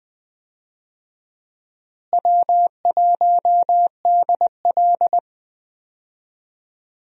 Morse code, heard as a keyed tone, about 20 wpm, 700 Hz. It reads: W1DL
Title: USA Call Signs - 4 Characters - Repeated 2x - 20wpm